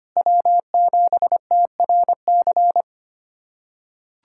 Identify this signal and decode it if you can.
Morse code, heard as a keyed tone, about 25 wpm, 700 Hz. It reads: W7TRC
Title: USA Callsign DE Callsign - 5 Characters - 25wpm